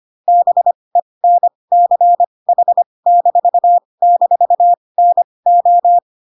Morse code, heard as a keyed tone, about 25 wpm, spaced as for 20 wpm, 700 Hz. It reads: BENCH--NO
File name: Pinocchio / Pinocchio ch1 15 25 ik1hge